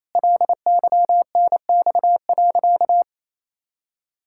Morse code, heard as a keyed tone, about 28 wpm, 700 Hz. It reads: LYNX.
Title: Pangram Challenge v2 - 28wpm